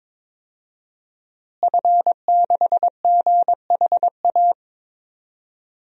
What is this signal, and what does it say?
Morse code, heard as a keyed tone, about 22 wpm, 700 Hz. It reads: F6GHA
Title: DXCC-100 Call Signs - 5 Characters - Repeated 2x - 22wpm